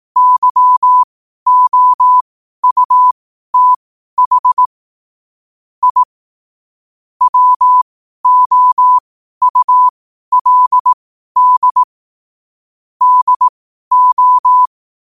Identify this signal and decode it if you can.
Morse code, heard as a keyed tone, about 18 wpm, spaced as for 12 wpm, 1000 Hz. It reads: YOUTH I WOULD DO
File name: qotdfiles / 12wpm